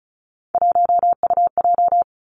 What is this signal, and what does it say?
Morse code, heard as a keyed tone, about 36 wpm, 700 Hz. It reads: 1UJ
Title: Three Challenging Letter-Number Combinations 35wpm